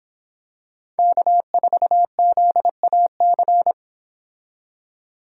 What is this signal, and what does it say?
Morse code, heard as a keyed tone, about 26 wpm, 700 Hz. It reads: K4ZAC